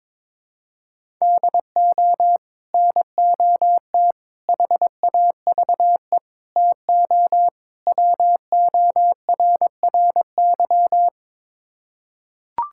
Morse code, heard as a keyed tone, about 22 wpm, 700 Hz. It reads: DO NOT HAVE TO WORRY
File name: practice / 22wpm / Build A Phrase - 5-Word Phrases - 22wpm